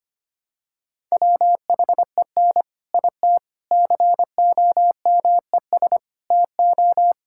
Morse code, heard as a keyed tone, about 25 wpm, 700 Hz. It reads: WHEN IT COMES TO
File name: practice / 25wpm / Sentences with 3-Word Phrases - 25wpm